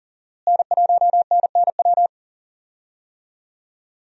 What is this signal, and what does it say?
Morse code, heard as a keyed tone, about 40 wpm, 690 Hz. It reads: N1NNW
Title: USA Call Signs - 5 Characters - Repeated 3x - 40wpm